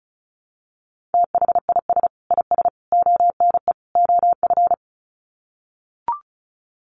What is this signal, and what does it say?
Morse code, heard as a keyed tone, about 34 wpm, 700 Hz. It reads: THIS IS ONE OF
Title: Sets of 4 Words 35wpm